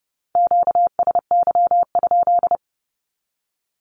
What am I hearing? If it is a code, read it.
Morse code, heard as a keyed tone, about 30 wpm, 700 Hz. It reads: QSY?
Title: Q-Codes 30wpm